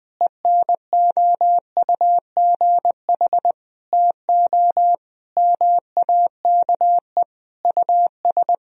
Morse code, heard as a keyed tone, about 20 wpm, 700 Hz. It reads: ENOUGH TO MAKE US
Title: Sentences from Top 1000 Words - Review Part 4 - 20wpm